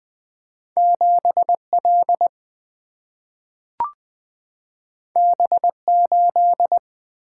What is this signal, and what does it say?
Morse code, heard as a keyed tone, about 20 wpm, 700 Hz. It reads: 7L B8
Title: Single-Letter Single-Number 20wpm